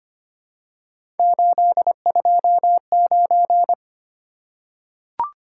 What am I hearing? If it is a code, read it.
Morse code, heard as a keyed tone, about 25 wpm, 700 Hz. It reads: 829
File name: Three Letter-Number Combinations - 25wpm